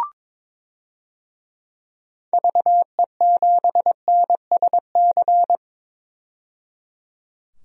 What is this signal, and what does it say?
Morse code, heard as a keyed tone, about 22 wpm, 700 Hz. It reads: VE7NSC
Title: DXCC-100 Call Signs - 6 Characters - 22wpm